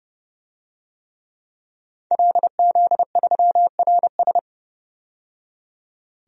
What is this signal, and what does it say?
Morse code, heard as a keyed tone, about 30 wpm, 700 Hz. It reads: LZ3RS
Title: DXCC-100 Call Signs - 5 Characters - 30wpm